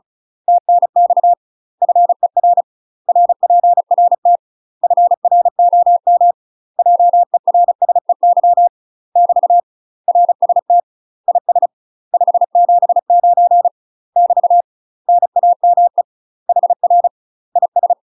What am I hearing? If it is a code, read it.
Morse code, heard as a keyed tone, about 34 wpm, 700 Hz. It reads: TNX FER RPRT FROM JERSEY <BT> RST IS 579 <BT> NAME HR IS
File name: UK MASTER QSOs TRAINING / 35WPM / UK MASTER QSOs 0043